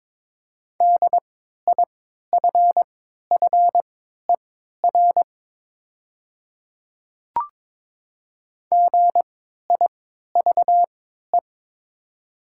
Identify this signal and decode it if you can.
Morse code, heard as a keyed tone, about 22 wpm, spaced as for 13 wpm, 700 Hz. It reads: DIFFER GIVE
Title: 3x Character Spacing - Top 300 Words - 22wpm